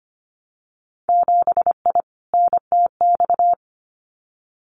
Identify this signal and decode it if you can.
Morse code, heard as a keyed tone, about 25 wpm, 700 Hz. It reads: 7I NTX
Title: Winter Field Day Exchanges Encore 25wpm